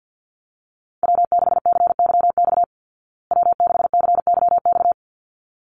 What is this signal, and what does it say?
Morse code, heard as a keyed tone, about 48 wpm, 700 Hz. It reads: W6CYX W6CYX
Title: Call Signs Repeated 3x - 50wpm